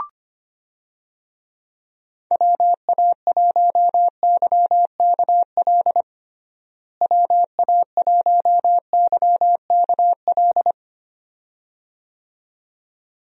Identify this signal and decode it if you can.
Morse code, heard as a keyed tone, about 25 wpm, 700 Hz. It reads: WA1YKL WA1YKL
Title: USA Call Signs - 6 Characters - Repeated 2x - 25wpm